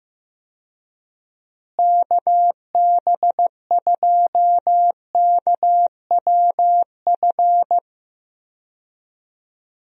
Morse code, heard as a keyed tone, about 15 wpm, 700 Hz. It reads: KB2KWF
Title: USA Call Signs - 6 Characters - Repeated 2x - 15wpm